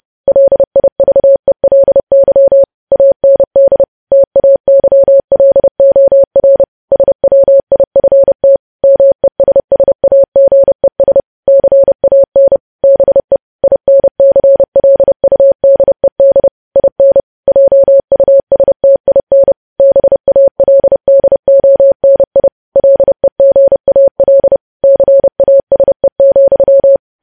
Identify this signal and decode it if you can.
Morse code, heard as a keyed tone, about 30 wpm, 550 Hz. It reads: LIVELY AND TAYLOR SWIFT MESSAGES CAN BE INCLUDED IN JUSTIN BALDONI LEGAL CASE,
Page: news headlines in morse code at 30 wpm